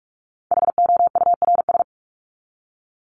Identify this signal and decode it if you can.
Morse code, heard as a keyed tone, about 44 wpm, 700 Hz. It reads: HOURS
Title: ES:S Suffix - Top 100 Words with Base Words 45wpm